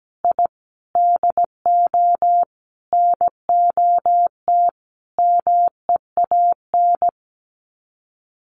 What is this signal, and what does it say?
Morse code, heard as a keyed tone, about 17 wpm, 700 Hz. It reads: I DO NOT MEAN
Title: Sets of 5 Words Encore2 17wpm